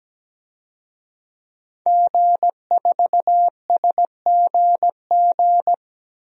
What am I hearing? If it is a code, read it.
Morse code, heard as a keyed tone, about 17 wpm, 700 Hz. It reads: G4SGG
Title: DXCC-100 Call Signs - 5 Characters - 17wpm